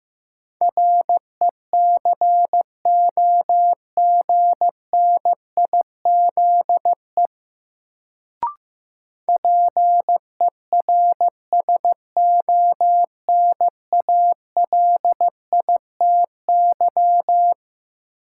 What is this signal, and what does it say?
Morse code, heard as a keyed tone, about 15 wpm, 700 Hz. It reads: RECOGNIZE PERSONALITY
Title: Sentences from Top 1400 Words - New Word Drill Alt - 15wpm